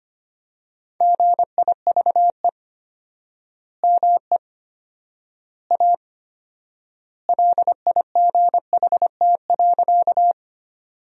Morse code, heard as a keyed tone, about 25 wpm, 700 Hz. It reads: GIVE ME A LIGHT.
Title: 4x Word-Spacing - Sentences from Top 200 Words - 25wpm